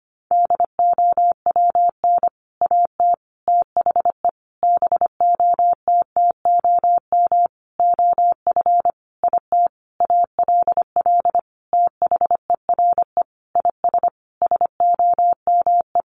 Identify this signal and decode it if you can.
Morse code, heard as a keyed tone, about 25 wpm, 700 Hz. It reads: DOWN AT THE BOTTOM OF IT ALL THERE IS SOME
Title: Sentences from Top 1100 Words 25wpm